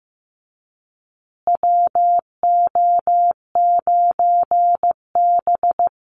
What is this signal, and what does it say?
Morse code, heard as a keyed tone, about 15 wpm, 700 Hz. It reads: WO9B
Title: Call Signs Repeated 3x - 15wpm